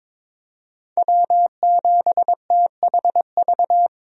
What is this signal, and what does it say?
Morse code, heard as a keyed tone, about 22 wpm, 700 Hz. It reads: W7THV
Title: USA Call Signs - 5 Characters - Repeated 2x - 22wpm